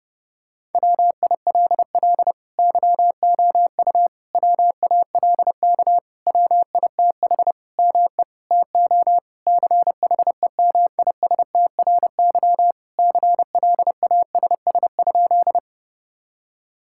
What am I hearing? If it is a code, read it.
Morse code, heard as a keyed tone, about 32 wpm, 700 Hz. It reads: WILL YOU WALK WITH ME TO CHEMISTRY CLASS?
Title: Sentences from Top 1000 Words - Review Part 6 - 30wpm